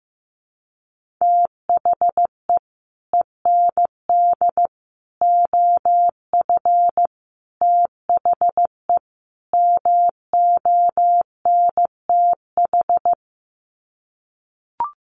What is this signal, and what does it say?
Morse code, heard as a keyed tone, about 15 wpm, 700 Hz. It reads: THE END OF THE MONTH
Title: Sets of 5 Words Encore 15wpm